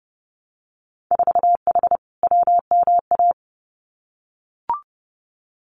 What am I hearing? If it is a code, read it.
Morse code, heard as a keyed tone, about 31 wpm, 700 Hz. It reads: 4H WMA
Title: Winter Field Day Exchanges Encore 30wpm